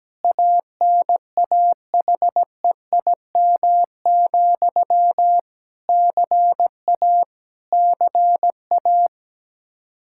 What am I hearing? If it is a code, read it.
Morse code, heard as a keyed tone, about 17 wpm, 700 Hz. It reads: ANAHEIM, CA CA
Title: Practice QSO - Exchange 1 - Beginning - 17wpm